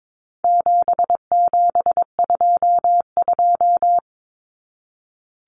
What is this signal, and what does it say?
Morse code, heard as a keyed tone, about 22 wpm, 700 Hz. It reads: 7722
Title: Numbers - 4-Digit 22wpm